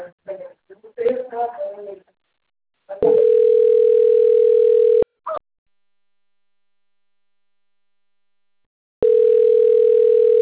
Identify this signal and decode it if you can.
Ringback tone (North America), 440 + 480 Hz, with speech.
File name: rg-1600-unknown- - - .72853